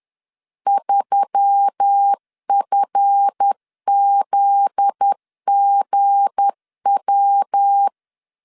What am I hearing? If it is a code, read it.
Morse code, heard as a keyed tone, about 10 wpm, 800 Hz. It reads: 3FZGW
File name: LEZIONI / 7 GF9Z / 7 GF9Z 10wpm 2611